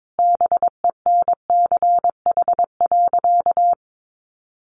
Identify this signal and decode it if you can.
Morse code, heard as a keyed tone, about 22 wpm, 700 Hz. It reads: BENCH.
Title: Sentences from Top 1400 Words 22wpm